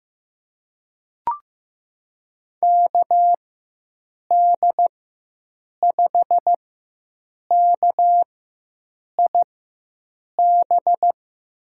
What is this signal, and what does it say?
Morse code, heard as a keyed tone, about 15 wpm, spaced as for 7 wpm, 700 Hz. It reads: KD5KIB
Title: USA Call Signs - 6 Characters - 4x Character Spacing - 15wpm